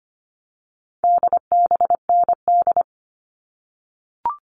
Morse code, heard as a keyed tone, about 25 wpm, 700 Hz. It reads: DBND
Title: All Cut Numbers - 4-Digit 25wpm